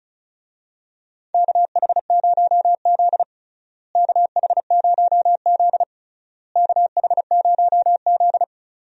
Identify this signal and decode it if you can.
Morse code, heard as a keyed tone, about 36 wpm, 700 Hz. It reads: KH0Z KH0Z KH0Z
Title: USA Call Signs - 4 Characters - Repeated 3x - 35wpm